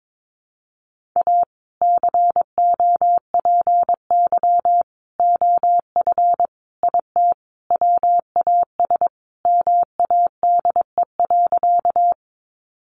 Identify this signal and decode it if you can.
Morse code, heard as a keyed tone, about 22 wpm, 700 Hz. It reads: A COPY OF IT WAS MADE.
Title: Sentences from Top 1400 Words 22wpm